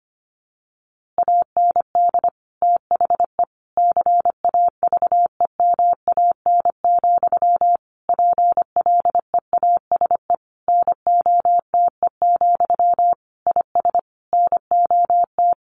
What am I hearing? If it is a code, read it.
Morse code, heard as a keyed tone, about 25 wpm, 700 Hz. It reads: AND THE CAVEMAN, PLEASE NOTE, IS NOT
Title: A Sound Of Thunder 25wpm